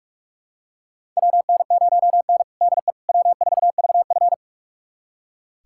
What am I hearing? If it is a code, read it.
Morse code, heard as a keyed tone, about 46 wpm, 700 Hz. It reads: WN0N DE W4VF
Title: USA Callsign DE Callsign - 4 Characters - 45wpm